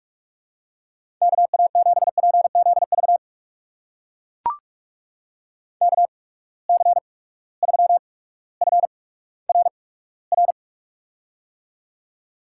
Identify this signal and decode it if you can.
Morse code, heard as a keyed tone, about 44 wpm, spaced as for 19 wpm, 700 Hz. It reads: KA7PZV KC3FRR
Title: USA Call Signs - 6 Characters - 8x Character Spacing - 45wpm